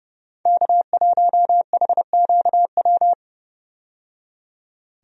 Morse code, heard as a keyed tone, about 29 wpm, 700 Hz. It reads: K1HQW
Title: USA Call Signs - 5 Characters - 30wpm